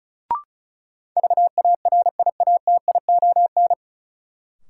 Morse code, heard as a keyed tone, about 34 wpm, 700 Hz. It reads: VARIATION
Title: Sentences from Top 1600 Words - New Word Drill - 35wpm